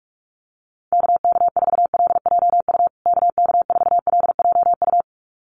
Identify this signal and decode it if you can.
Morse code, heard as a keyed tone, about 44 wpm, 700 Hz. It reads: KK4LJU KK4LJU
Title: USA Call Signs Repeated 2x 45wpm